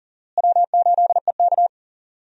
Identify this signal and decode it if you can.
Morse code, heard as a keyed tone, about 37 wpm, 700 Hz. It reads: W8EK
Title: USA Call Signs - 4 Characters - Repeated 2x - 40wpm